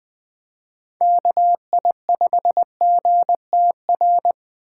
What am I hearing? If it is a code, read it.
Morse code, heard as a keyed tone, about 20 wpm, 700 Hz. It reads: KI5GTR
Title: 2023 POTA Hunter Call Signs - 20wpm